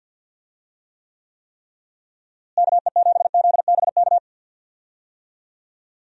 Morse code, heard as a keyed tone, about 52 wpm, 700 Hz. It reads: KE7ZBK
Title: USA Call Signs - 6 Characters - Repeated 3x - 50wpm